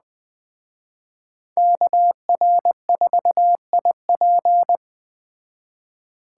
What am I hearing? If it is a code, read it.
Morse code, heard as a keyed tone, about 19 wpm, 700 Hz. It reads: KR4IP